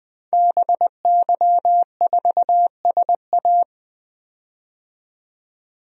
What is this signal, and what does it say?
Morse code, heard as a keyed tone, about 20 wpm, 700 Hz. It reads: BY4SA
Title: DXCC-100 Call Signs - 5 Characters - 20wpm